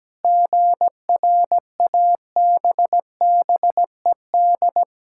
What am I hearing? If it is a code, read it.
Morse code, heard as a keyed tone, about 17 wpm, 700 Hz. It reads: GRABBED
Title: Pangram Challenge v2 - 17wpm